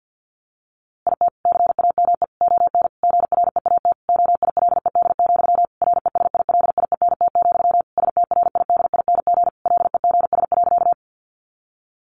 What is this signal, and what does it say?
Morse code, heard as a keyed tone, about 50 wpm, 700 Hz. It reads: IT CAME ON GREAT OILED, RESILIENT, STRIDING LEGS.
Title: A Sound Of Thunder 50wpm